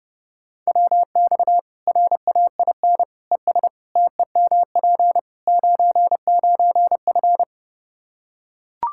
Morse code, heard as a keyed tone, about 31 wpm, 700 Hz. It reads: WX RAIN ES TEMP 99F
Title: Practice QSO - Exchange 2 - Info - 30wpm